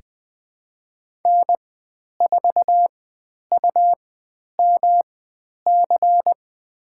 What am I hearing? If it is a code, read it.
Morse code, heard as a keyed tone, about 20 wpm, spaced as for 10 wpm, 700 Hz. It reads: N4UMC
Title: 109 Lesson 18g - Letter m - Call Signs 20wpm